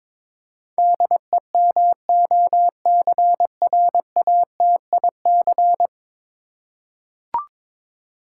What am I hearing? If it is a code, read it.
Morse code, heard as a keyed tone, about 21 wpm, 700 Hz. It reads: DEMOCRATIC
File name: practice / 22wpm / 10 Letter Words - First 3 Letters Given - 22wpm